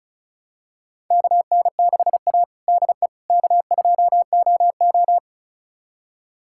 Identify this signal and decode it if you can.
Morse code, heard as a keyed tone, about 36 wpm, 700 Hz. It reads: KN6A DE K2OO
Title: USA Callsign DE Callsign - 4 Characters - 35wpm